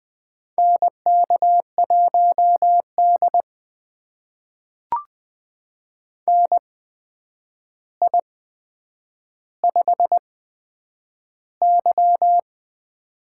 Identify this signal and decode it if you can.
Morse code, heard as a keyed tone, about 20 wpm, spaced as for 9 wpm, 700 Hz. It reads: NK1D NI5Y
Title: USA Call Signs - 4 Characters - 8x Character Spacing - 20wpm